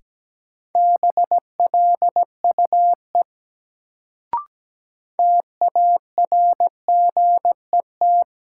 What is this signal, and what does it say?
Morse code, heard as a keyed tone, about 17 wpm, 700 Hz. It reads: BLUE TARGET
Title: Sentences from Top 1200 Words - New Word Drill Alt - 17wpm